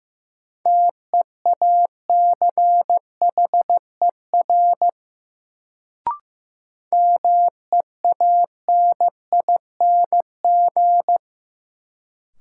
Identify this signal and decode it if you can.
Morse code, heard as a keyed tone, about 15 wpm, 700 Hz. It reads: TEACHER MEANING
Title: Sentences from Top 900 Words - New Word Drill - 15wpm